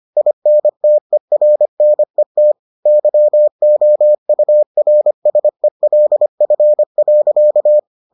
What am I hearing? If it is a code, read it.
Morse code, heard as a keyed tone, about 25 wpm, 600 Hz. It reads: INTERNET YOURSELF.